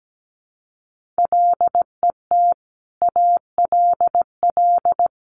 Morse code, heard as a keyed tone, about 17 wpm, 700 Hz. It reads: LET ALL
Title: Sentences from Top 1500 Words 17wpm